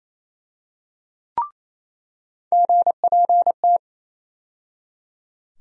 Morse code, heard as a keyed tone, about 28 wpm, 700 Hz. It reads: GPT
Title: USA Call Sign Postfixes - 3 Letters 28wpm